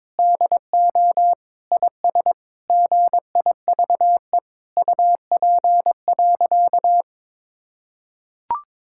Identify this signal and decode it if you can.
Morse code, heard as a keyed tone, about 22 wpm, 700 Hz. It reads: DO IS GIVE UP.
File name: practice / 22wpm / Sentences with 8-Word Phrases - 22wpm